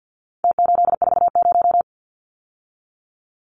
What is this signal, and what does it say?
Morse code, heard as a keyed tone, about 50 wpm, 700 Hz. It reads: T840
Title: Numbers - Contest Style - 4-Digit 50wpm